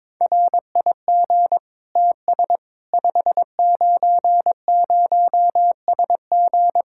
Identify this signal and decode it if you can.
Morse code, heard as a keyed tone, about 22 wpm, 700 Hz. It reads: RIG TS 590SG
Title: K1USN SST QSO Practice 22wpm